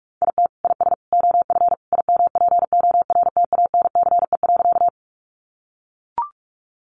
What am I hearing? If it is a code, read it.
Morse code, heard as a keyed tone, about 44 wpm, 700 Hz. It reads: IT IS OF IMPORTANCE.
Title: Sentences from Top 1200 Words 45wpm